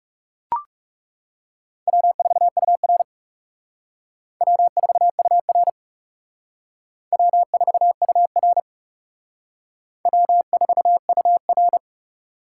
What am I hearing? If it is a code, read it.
Morse code, heard as a keyed tone, about 36 wpm, 700 Hz. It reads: W4UR W4UR W4UR W4UR
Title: Speed Racing - USA Call Signs 30wpm